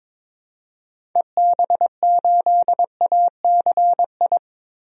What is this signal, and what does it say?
Morse code, heard as a keyed tone, about 21 wpm, 700 Hz. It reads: EB8ACI
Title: DXCC-100 Call Signs - 6 Characters - 22wpm